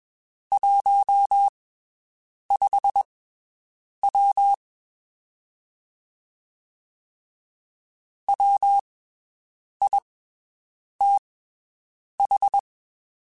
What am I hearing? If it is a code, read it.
Morse code, heard as a keyed tone, about 21 wpm, spaced as for 7 wpm, 800 Hz. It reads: 15W WITH
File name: (spaced) / QSOs Spaced-mp3s / WD0BGP 1 @20 sp